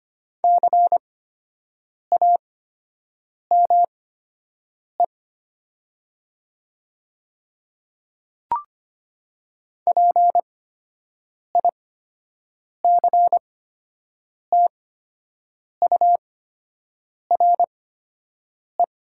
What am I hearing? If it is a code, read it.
Morse code, heard as a keyed tone, about 25 wpm, spaced as for 7 wpm, 700 Hz. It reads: CAME PICTURE
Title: 8x Character Spacing - Top 200 Words - 25wpm